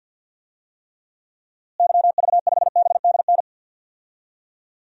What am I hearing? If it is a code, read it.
Morse code, heard as a keyed tone, about 49 wpm, 700 Hz. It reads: YV5BDN
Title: DXCC-100 Call Signs - 6 Characters - Repeated 2x - 50wpm